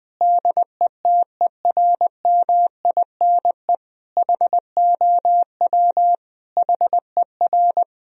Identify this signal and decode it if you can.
Morse code, heard as a keyed tone, about 20 wpm, 700 Hz. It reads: DETERMINE HOW HER
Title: Sentences from Top 1000 Words - Review Part 4 - 20wpm